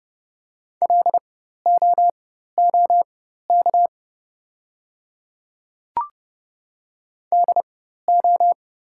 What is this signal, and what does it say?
Morse code, heard as a keyed tone, about 30 wpm, spaced as for 14 wpm, 690 Hz. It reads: LOOK DO
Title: 4x Character Spacing - Top 100 Words - 30wpm